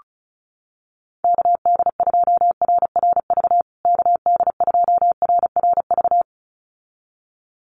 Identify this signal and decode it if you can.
Morse code, heard as a keyed tone, about 34 wpm, 700 Hz. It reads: KD2RRV KD2RRV
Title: USA Call Signs Repeated 2x 35wpm